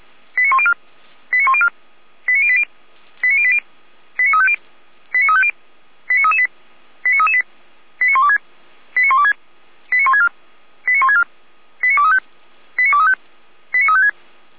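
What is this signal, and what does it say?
Five-tone selective call (ZVEI), heard as quick runs of five tones, tones of about 70 ms. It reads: 89104 89104 89080 89080 89350 89350 89338 89338 89126 89126 89164 89164 89235 89235 89356
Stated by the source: Five-tone ZVEI selective call received by SDR.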